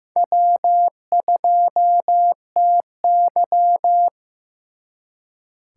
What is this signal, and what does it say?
Morse code, heard as a keyed tone, about 18 wpm, 700 Hz. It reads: W2TY
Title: Speed Racing - USA 1x2 Call Signs 15wpm